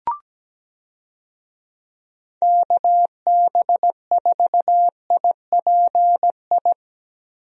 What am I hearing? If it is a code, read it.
Morse code, heard as a keyed tone, about 17 wpm, 700 Hz. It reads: KB4IPI
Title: USA Call Signs - 6 Characters - Repeated 2x - 17wpm